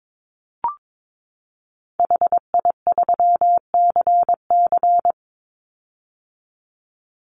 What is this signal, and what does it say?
Morse code, heard as a keyed tone, about 22 wpm, 700 Hz. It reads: HI3CC
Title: Call Signs - 22wpm